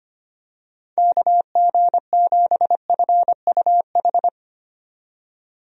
Morse code, heard as a keyed tone, about 25 wpm, 700 Hz. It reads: KG7FUH
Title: USA Call Signs - 6 Characters - 25wpm